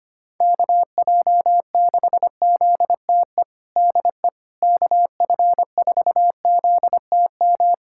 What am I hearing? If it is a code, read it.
Morse code, heard as a keyed tone, about 25 wpm, 700 Hz. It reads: KJ6ZTE DE KF4ZTM
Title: USA Callsign DE Callsign - 6 Characters - 25wpm